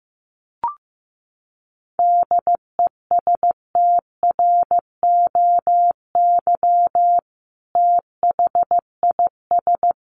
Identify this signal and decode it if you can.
Morse code, heard as a keyed tone, about 15 wpm, 700 Hz. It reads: DESTROY THIS
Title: A Sound Of Thunder 15wpm